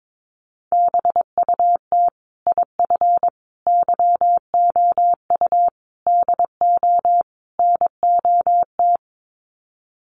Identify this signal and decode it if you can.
Morse code, heard as a keyed tone, about 22 wpm, 700 Hz. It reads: BUT IF YOU DO NOT
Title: Sets of 5 Words Encore 22wpm